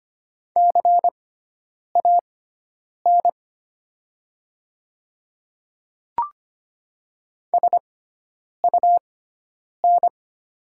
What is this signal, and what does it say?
Morse code, heard as a keyed tone, about 25 wpm, spaced as for 9 wpm, 700 Hz. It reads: CAN SUN
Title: 6x Character Spacing - Top 500 Words - 25wpm